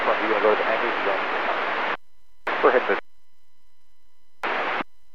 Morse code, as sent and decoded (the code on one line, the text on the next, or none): none